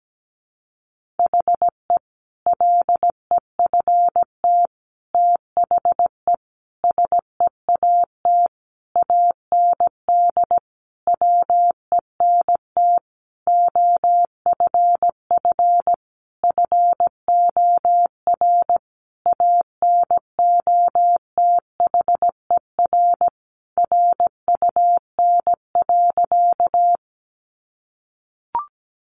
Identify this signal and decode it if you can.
Morse code, heard as a keyed tone, about 17 wpm, 700 Hz. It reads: HE LEFT THE SEAT AND WENT OFF FOR ANOTHER RUN.
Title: Sentences from Top 1500 Words 17wpm